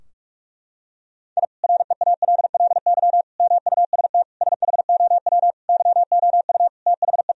Morse code, heard as a keyed tone, about 43 wpm, 700 Hz. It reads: I REALLY MUST SHOW YOU THE
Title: Sentences from Top 1000 Words - Review Part 4 - 45wpm